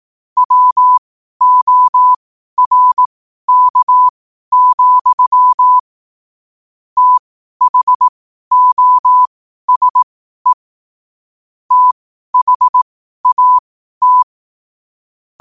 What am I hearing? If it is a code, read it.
Morse code, heard as a keyed tone, about 18 wpm, spaced as for 12 wpm, 1000 Hz. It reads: WORK, THOSE THAT